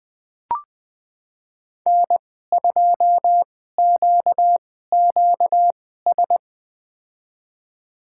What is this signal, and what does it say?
Morse code, heard as a keyed tone, about 20 wpm, spaced as for 14 wpm, 700 Hz. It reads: N2QQS
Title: USA Call Signs - 5 Characters - 2x Character Spacing - 20wpm